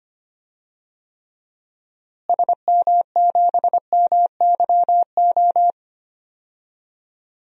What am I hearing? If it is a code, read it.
Morse code, heard as a keyed tone, about 25 wpm, 700 Hz. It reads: SM7MYO
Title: DXCC-100 Call Signs - 6 Characters - Repeated 2x - 25wpm